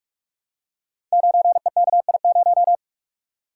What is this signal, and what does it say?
Morse code, heard as a keyed tone, about 44 wpm, 700 Hz. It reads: 9EKI0
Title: Code Groups - 5 Letter:Numbers - 45wpm